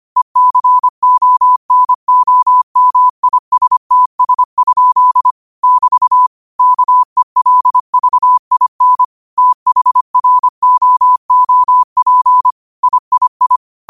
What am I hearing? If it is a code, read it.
Morse code, heard as a keyed tone, about 25 wpm, 1000 Hz. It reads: ECONOMISTS? <BT> KELVIN THROOP III
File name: 25wpm